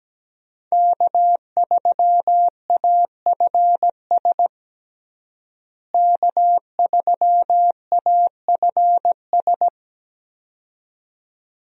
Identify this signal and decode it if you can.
Morse code, heard as a keyed tone, about 17 wpm, 700 Hz. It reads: K3AFS K3AFS
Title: USA Call Signs - 5 Characters - Repeated 2x - 17wpm